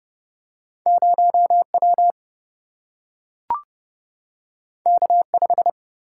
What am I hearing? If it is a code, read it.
Morse code, heard as a keyed tone, about 30 wpm, 700 Hz. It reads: 0 W K 5
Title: Single-Letter Single-Number 30wpm